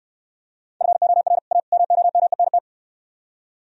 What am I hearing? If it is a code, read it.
Morse code, heard as a keyed tone, about 68 wpm, 700 Hz. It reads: FOR A MOMENT